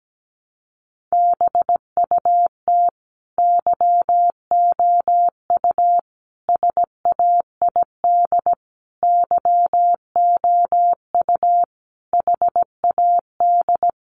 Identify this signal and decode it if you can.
Morse code, heard as a keyed tone, about 17 wpm, 700 Hz. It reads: BUT YOU SAID YOU HAD
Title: Sentences from Top 1400 Words 17wpm